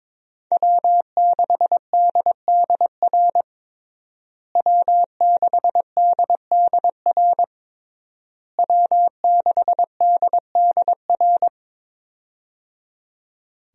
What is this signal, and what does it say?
Morse code, heard as a keyed tone, about 22 wpm, 700 Hz. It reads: W6DDR W6DDR W6DDR
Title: USA Call Signs - 5 Characters - Repeated 3x - 22wpm